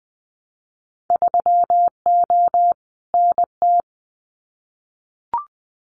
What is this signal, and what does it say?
Morse code, heard as a keyed tone, about 19 wpm, 700 Hz. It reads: 3O NT
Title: Winter Field Day Exchanges Encore 20wpm